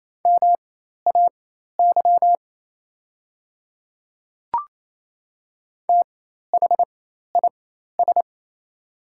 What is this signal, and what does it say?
Morse code, heard as a keyed tone, about 28 wpm, spaced as for 13 wpm, 690 Hz. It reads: MAY THIS
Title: 4x Character Spacing - Top 100 Words - 28wpm